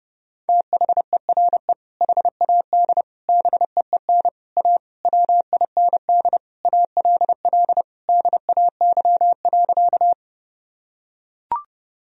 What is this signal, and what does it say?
Morse code, heard as a keyed tone, about 30 wpm, 700 Hz. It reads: THERE HAD BEEN A WIND ALL DAY.
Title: Sentences from Top 1000 Words - Review Part 2 - 30wpm